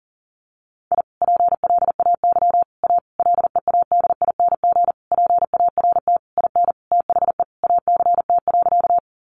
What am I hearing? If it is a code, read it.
Morse code, heard as a keyed tone, about 40 wpm, 700 Hz. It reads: I PLAY A LEADING PART IN THE ACT.
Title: Sentences from Top 1100 Words 40wpm